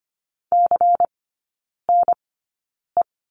Morse code, heard as a keyed tone, about 25 wpm, spaced as for 9 wpm, 700 Hz. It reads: CNE
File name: course / 25:10wpm / 093 Lesson 16e - Letter c - Three Characters 25wpm